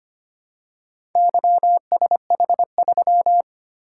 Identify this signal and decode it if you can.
Morse code, heard as a keyed tone, about 25 wpm, 700 Hz. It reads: YSH3
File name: practice / 25wpm / Four Letter-Number Combinations - 25wpm